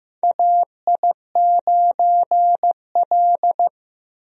Morse code, heard as a keyed tone, about 15 wpm, 700 Hz. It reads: AI9L
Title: USA Call Signs - 4 Characters - 15wpm